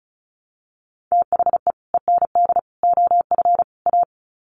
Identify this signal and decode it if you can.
Morse code, heard as a keyed tone, about 34 wpm, 700 Hz. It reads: THE END OF A
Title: Sets of 4 Words 35wpm